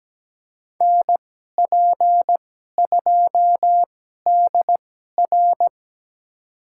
Morse code, heard as a keyed tone, about 17 wpm, spaced as for 12 wpm, 700 Hz. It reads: NP2DR
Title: USA Call Signs - 5 Characters - 2x Character Spacing - 17wpm